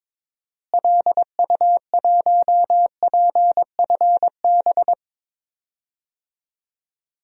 Morse code, heard as a keyed tone, about 22 wpm, 700 Hz. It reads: LU1PFB
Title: DXCC-100 Call Signs - 6 Characters - Repeated 2x - 22wpm